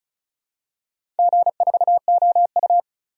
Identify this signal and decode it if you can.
Morse code, heard as a keyed tone, about 36 wpm, 700 Hz. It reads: G4OU
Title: DXCC-100 Call Signs - 4 Characters - 35wpm